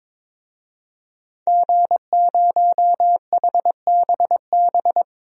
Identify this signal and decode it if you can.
Morse code, heard as a keyed tone, about 22 wpm, 700 Hz. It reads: G0HBB
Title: DXCC-100 Call Signs - 5 Characters - 22wpm